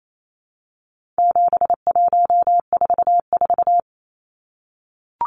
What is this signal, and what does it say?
Morse code, heard as a keyed tone, about 28 wpm, 700 Hz. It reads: 7144
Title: Numbers - 4-Digit 28wpm